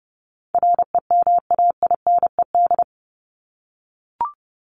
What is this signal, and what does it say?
Morse code, heard as a keyed tone, about 30 wpm, 700 Hz. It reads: REMAINED
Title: ED Suffix - Top 100 Words with Base Words 30wpm